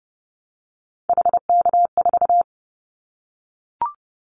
Morse code, heard as a keyed tone, about 30 wpm, 700 Hz. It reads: HK4
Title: Callsign Prefixes for Top 100 DXCC - 30wpm